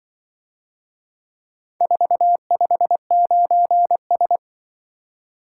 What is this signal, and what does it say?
Morse code, heard as a keyed tone, about 24 wpm, 700 Hz. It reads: 459S
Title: Speed-Racing - Four Letter-Number Combinations - 20wpm